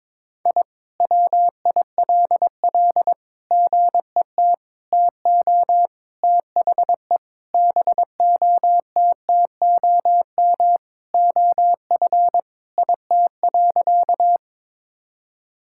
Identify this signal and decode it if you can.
Morse code, heard as a keyed tone, about 22 wpm, 700 Hz. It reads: I WILL GET TO THE BOTTOM OF IT.
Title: Sentences from Top 1000 Words - Review Part 6 - 22wpm